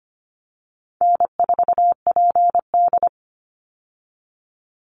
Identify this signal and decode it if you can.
Morse code, heard as a keyed tone, about 25 wpm, 700 Hz. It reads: N4PD
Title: Easy Call Signs Repeated 3x - 25wpm